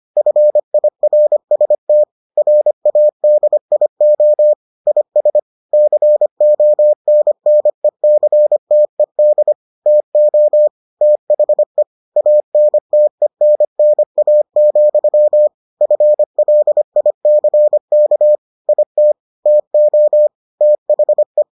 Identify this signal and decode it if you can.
Morse code, heard as a keyed tone, about 25 wpm, 600 Hz. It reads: FIRST RADIO IS CONNECTED TO THE ANTENNA, FLICK IT TO THE